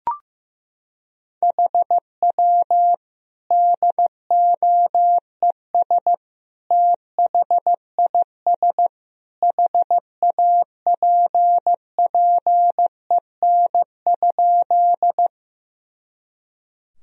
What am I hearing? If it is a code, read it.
Morse code, heard as a keyed tone, about 16 wpm, 700 Hz. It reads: HW DOES THIS HAPPEN?
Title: Sentences from Top 400 Words and Abbreviations 15wpm